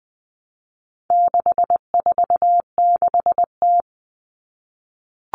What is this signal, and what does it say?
Morse code, heard as a keyed tone, about 20 wpm, 700 Hz. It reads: 646T
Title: All Cut Numbers - 4-Digit 20wpm